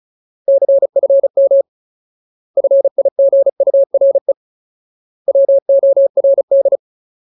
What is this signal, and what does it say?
Morse code, heard as a keyed tone, about 34 wpm, 550 Hz. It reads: CFM FIGURE WORD
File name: Ultimate Word List / 35WPM Ultimate